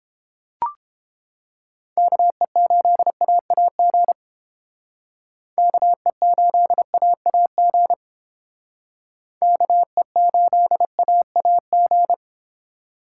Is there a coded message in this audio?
Yes: Morse code from 2.0 to 12.1 s, heard as a keyed tone, about 31 wpm, 700 Hz. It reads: KE8AAG KE8AAG KE8AAG